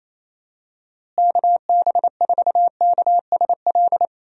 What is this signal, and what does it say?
Morse code, heard as a keyed tone, about 28 wpm, 700 Hz. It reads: KB4KSL
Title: USA Call Signs - 6 Characters - Repeated 3x - 28wpm